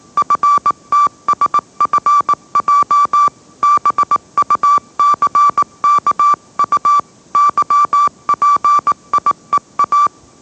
Morse code, heard as a keyed tone, about 18 wpm, 1160 Hz. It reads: FTSFJ BUCKU YPIEA
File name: mrx / mrx081 20wpm